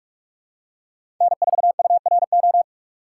Morse code, heard as a keyed tone, about 43 wpm, 700 Hz. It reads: N4URO
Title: USA Call Signs - 5 Characters - 4x Character Spacing - 45wpm